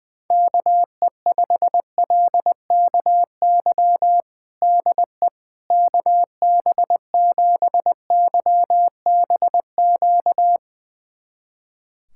Morse code, heard as a keyed tone, about 20 wpm, 700 Hz. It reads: KE5LKY DE KB7YBQ
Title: USA Callsign DE Callsign - 6 Characters - 20wpm